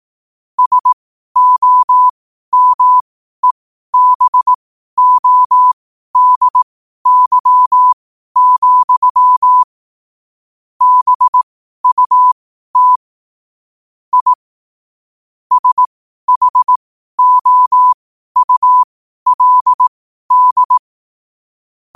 Morse code, heard as a keyed tone, about 18 wpm, spaced as for 12 wpm, 1000 Hz. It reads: SOMEBODY, BUT I SHOULD
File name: qotdfiles / 12wpm